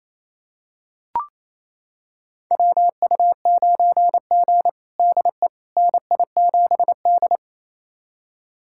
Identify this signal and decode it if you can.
Morse code, heard as a keyed tone, about 28 wpm, 700 Hz. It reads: WU9G DE NI7D
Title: USA Callsign DE Callsign - 4 Characters - 28wpm